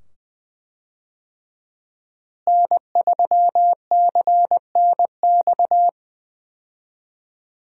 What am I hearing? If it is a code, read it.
Morse code, heard as a keyed tone, about 21 wpm, 700 Hz. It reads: N3CNX
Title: USA Call Signs - 5 Characters - Repeated 2x - 20wpm